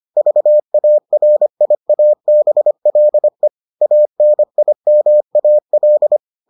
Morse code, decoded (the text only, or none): VARIABLE ANIMAL